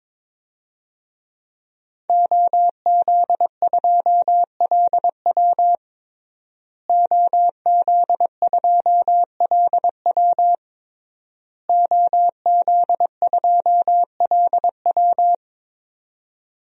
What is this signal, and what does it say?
Morse code, heard as a keyed tone, about 22 wpm, 700 Hz. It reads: OZ2LW OZ2LW OZ2LW
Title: DXCC-100 Call Signs - 5 Characters - Repeated 3x - 22wpm